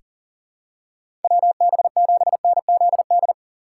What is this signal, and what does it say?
Morse code, heard as a keyed tone, about 40 wpm, 700 Hz. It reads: WB7NZD
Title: USA Call Signs - 6 Characters - 6x Character Spacing - 40wpm